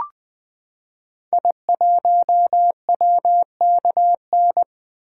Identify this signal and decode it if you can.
Morse code, heard as a keyed tone, about 20 wpm, 700 Hz. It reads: I1WKN
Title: 2023 SOTA Activator Call Signs - 20wpm